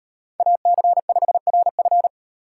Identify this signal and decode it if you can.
Morse code, heard as a keyed tone, about 36 wpm, 700 Hz. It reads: AC5RF